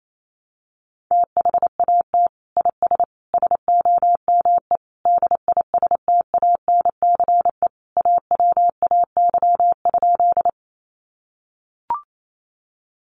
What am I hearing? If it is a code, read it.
Morse code, heard as a keyed tone, about 28 wpm, 700 Hz. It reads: THAT IS SOME DISTANCE AWAY?
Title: Sentences from Top 1200 Words 28wpm